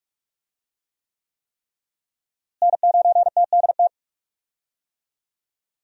Morse code, heard as a keyed tone, about 48 wpm, 700 Hz. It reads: N9TDT
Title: USA Call Signs - 5 Characters - Repeated 3x - 45wpm